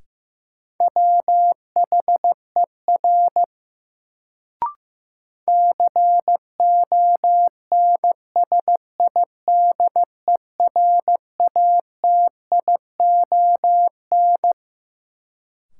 Morse code, heard as a keyed tone, about 15 wpm, 700 Hz. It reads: WHER CONSIDERATION
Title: Sentences from Top 1500 Words - New Word Drill Alt - 15wpm